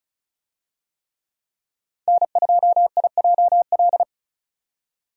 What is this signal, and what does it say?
Morse code, heard as a keyed tone, about 36 wpm, 700 Hz. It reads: N2IJL
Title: USA Call Signs - 5 Characters - 35wpm